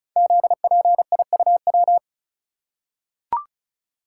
Morse code, heard as a keyed tone, about 36 wpm, 700 Hz. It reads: ZPIUW
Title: Code Groups - 5 Letters - 35wpm